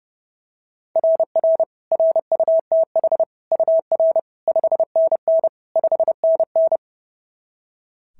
Morse code, heard as a keyed tone, about 30 wpm, 660 Hz. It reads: RR RUTH UR 5NN 5NN
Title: SOTA QSO Practice 30wpm